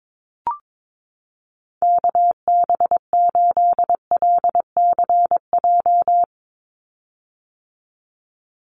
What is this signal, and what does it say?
Morse code, heard as a keyed tone, about 22 wpm, 700 Hz. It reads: KB8LCJ
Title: USA Call Signs 22wpm